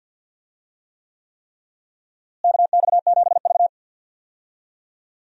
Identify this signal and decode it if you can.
Morse code, heard as a keyed tone, about 52 wpm, 700 Hz. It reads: KX7V
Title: USA Call Signs - 4 Characters - Repeated 3x - 50wpm